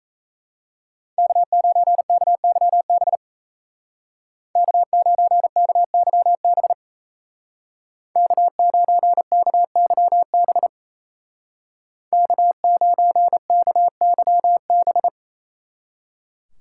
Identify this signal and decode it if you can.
Morse code, heard as a keyed tone, about 37 wpm, 700 Hz. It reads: K9KYB K9KYB K9KYB K9KYB